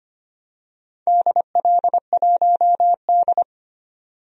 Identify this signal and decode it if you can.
Morse code, heard as a keyed tone, about 25 wpm, 700 Hz. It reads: DL1D
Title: DXCC-100 Call Signs - 4 Characters - 25wpm